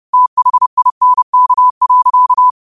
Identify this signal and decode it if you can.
Morse code, heard as a keyed tone, about 30 wpm, 1000 Hz. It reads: THINK.
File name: qotdfiles / 30wpm